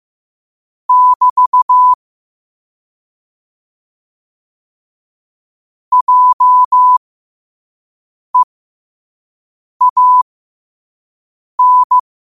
Morse code, heard as a keyed tone, about 15 wpm, spaced as for 5 wpm, 1000 Hz. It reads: <BT> JEAN
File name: qotdfiles / 5wpm